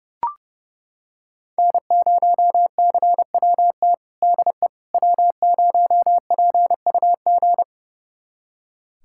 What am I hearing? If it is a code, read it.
Morse code, heard as a keyed tone, about 30 wpm, 700 Hz. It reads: N0CWT DE W0PUG
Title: USA Callsign DE Callsign - 5 Characters - 30wpm